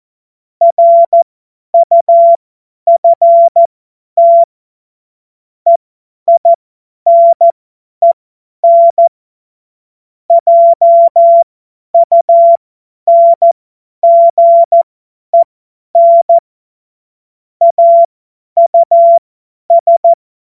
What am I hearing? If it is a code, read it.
Morse code, heard as a keyed tone, about 14 wpm, spaced as for 10 wpm, 680 Hz. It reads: RUFT EINEN JUNGEN AUS
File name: Morsekurs / Witze / Witz 04. Oktober 2022 50 F70